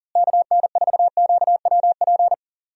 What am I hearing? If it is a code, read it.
Morse code, heard as a keyed tone, about 40 wpm, 700 Hz. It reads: KN4QWP
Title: USA Call Signs - 6 Characters - Repeated 3x - 40wpm